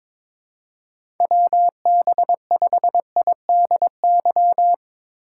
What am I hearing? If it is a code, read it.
Morse code, heard as a keyed tone, about 22 wpm, 700 Hz. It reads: WB5IDY
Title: K1USN SST QSO Practice - 22wpm